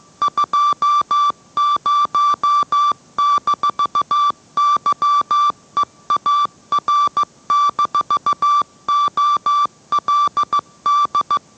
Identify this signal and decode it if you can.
Morse code, heard as a keyed tone, about 15 wpm, 1170 Hz. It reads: 20-YEAR-OLD